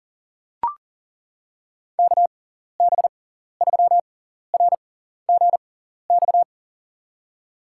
Morse code, heard as a keyed tone, about 38 wpm, spaced as for 14 wpm, 700 Hz. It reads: KB3RGX
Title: USA Call Signs - 6 Characters - 6x Character Spacing - 40wpm